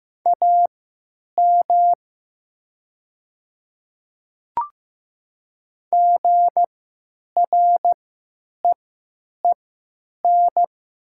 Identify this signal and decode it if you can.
Morse code, heard as a keyed tone, about 16 wpm, spaced as for 9 wpm, 700 Hz. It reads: AM GREEN
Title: 3x Character Spacing - Top 500 Words - 15wpm